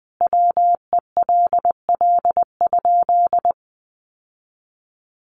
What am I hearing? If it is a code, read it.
Morse code, heard as a keyed tone, about 19 wpm, 700 Hz. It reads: WELL?
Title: Sentences from Top 1400 Words 20wpm